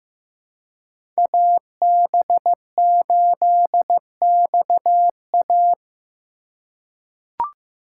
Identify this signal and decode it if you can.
Morse code, heard as a keyed tone, about 16 wpm, 700 Hz. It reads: AB8XA
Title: USA Call Signs - 5 Characters - 2x Character Spacing - 15wpm